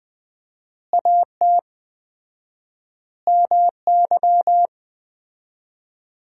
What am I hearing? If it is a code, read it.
Morse code, heard as a keyed tone, about 19 wpm, 700 Hz. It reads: AT MY
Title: 4x Word-Spacing - Sentences from Top 200 Words - 20wpm